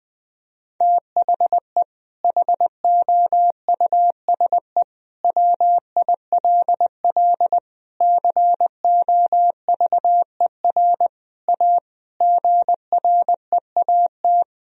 Morse code, heard as a keyed tone, about 21 wpm, 700 Hz. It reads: THE HOUSE WILL COVER A GREAT
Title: Sentences from Top 1000 Words - Review Part 2 - 20wpm